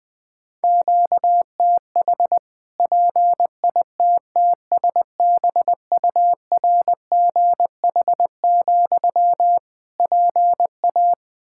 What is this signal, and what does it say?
Morse code, heard as a keyed tone, about 20 wpm, 700 Hz. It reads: QTH PITTSBURGH, PA
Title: Practice QSO - Exchange 1 - Beginning - 20wpm